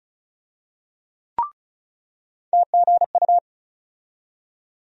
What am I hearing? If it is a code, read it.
Morse code, heard as a keyed tone, about 34 wpm, 700 Hz. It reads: TGU
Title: USA Call Sign Postfixes - 3 Letters 35wpm